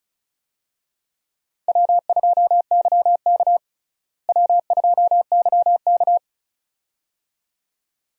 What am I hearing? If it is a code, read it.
Morse code, heard as a keyed tone, about 36 wpm, 700 Hz. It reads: W2YK W2YK
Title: USA Call Signs - 4 Characters - Repeated 2x - 35wpm